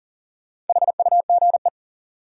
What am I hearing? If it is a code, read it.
Morse code, heard as a keyed tone, about 40 wpm, 700 Hz. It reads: HUGE
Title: Sentences from Top 1100 Words - New Word Drill Alt - 40wpm